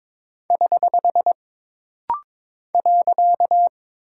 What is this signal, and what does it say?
Morse code, heard as a keyed tone, about 22 wpm, 700 Hz. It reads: <HH> .
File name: practice / 22wpm / Single Punctuation-Prosign - Rapid-Fire - 22wpm